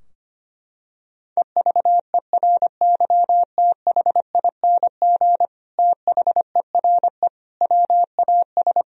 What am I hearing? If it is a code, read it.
Morse code, heard as a keyed tone, about 25 wpm, 700 Hz. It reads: EVERYTHING THERE WAS
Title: Sentences from Top 1000 Words - Review Part 2 - 25wpm